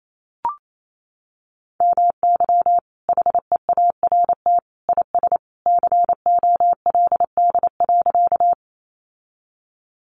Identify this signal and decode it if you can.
Morse code, heard as a keyed tone, about 28 wpm, 700 Hz. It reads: MY HEART IS COLD.
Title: Sentences from Top 1200 Words 28wpm